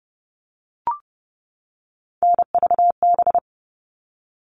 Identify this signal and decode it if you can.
Morse code, heard as a keyed tone, about 30 wpm, 700 Hz. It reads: NVB
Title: All Cut Numbers - 3-Digit 30wpm